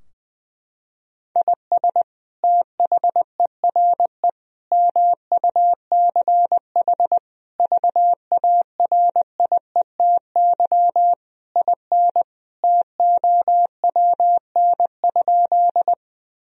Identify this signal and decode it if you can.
Morse code, heard as a keyed tone, about 19 wpm, 700 Hz. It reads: IS THERE MUCH VARIETY IN TOWN?
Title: Sentences from Top 1000 Words - Review Part 4 - 20wpm